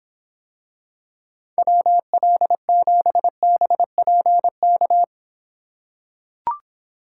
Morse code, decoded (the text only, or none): WL7BPK